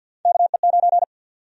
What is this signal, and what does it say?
Morse code, heard as a keyed tone, about 52 wpm, 700 Hz. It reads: KE9
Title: USA Call Sign Prefixes - Repeated 2x - 50wpm